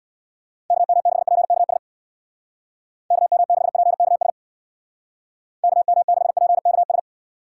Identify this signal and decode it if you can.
Morse code, heard as a keyed tone, about 61 wpm, 700 Hz. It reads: KM6PKS KM6PKS KM6PKS
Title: Speed Racing - USA Call Signs 50wpm